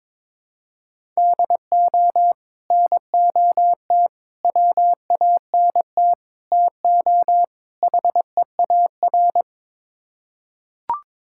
Morse code, heard as a keyed tone, about 21 wpm, 700 Hz. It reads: DO NOT WANT TO HEAR
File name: practice / 22wpm / Build A Phrase - 5-Word Phrases - 22wpm